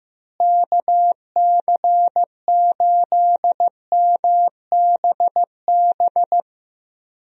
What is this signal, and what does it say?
Morse code, heard as a keyed tone, about 16 wpm, 700 Hz. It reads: KC8MBB
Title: USA Call Signs - 6 Characters - Repeated 2x - 15wpm